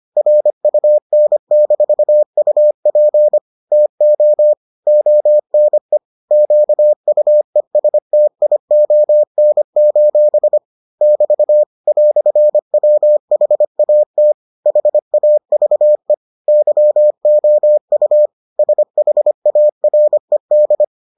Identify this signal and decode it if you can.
Morse code, heard as a keyed tone, about 25 wpm, 600 Hz. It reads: RUN-UP TO ONE QUESTION: <BT> "WHAT HAVE YOU SHARED